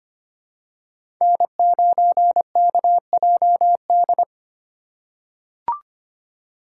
Morse code, heard as a keyed tone, about 25 wpm, 700 Hz. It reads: N9KJD
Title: USA Call Signs - 4x Character Spacing - 25wpm